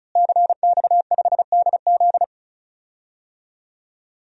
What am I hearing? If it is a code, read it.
Morse code, heard as a keyed tone, about 34 wpm, 700 Hz. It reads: CX5DZ
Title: DXCC-100 Call Signs - 5 Characters - Repeated 3x - 35wpm